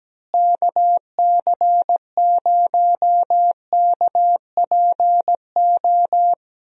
Morse code, heard as a keyed tone, about 17 wpm, 700 Hz. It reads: KC0KPO
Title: USA Call Signs - 6 Characters - Repeated 3x - 17wpm